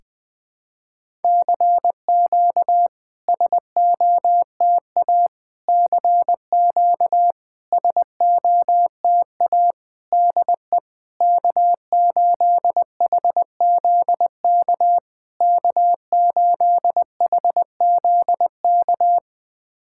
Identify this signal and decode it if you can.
Morse code, heard as a keyed tone, about 21 wpm, 700 Hz. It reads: CQ SOTA CQ SOTA DE K8HZK K8HZK
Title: SOTA QSO Practice 20wpm